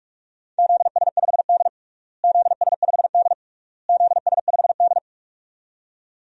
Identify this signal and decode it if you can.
Morse code, heard as a keyed tone, about 44 wpm, 700 Hz. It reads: ZS5D ZS5D ZS5D
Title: DXCC-100 Call Signs - 4 Characters - Repeated 3x - 45wpm